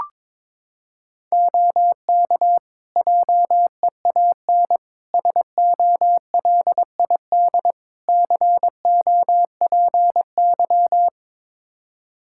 Morse code, heard as a keyed tone, about 22 wpm, 700 Hz. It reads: OK JEAN SOLID COPY
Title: Practice QSO - Exchange 3 - More Info - 22wpm